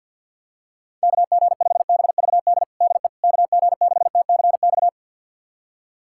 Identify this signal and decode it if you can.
Morse code, heard as a keyed tone, about 50 wpm, 700 Hz. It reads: KG5BVD DE KG6TCX
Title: USA Callsign DE Callsign 50wpm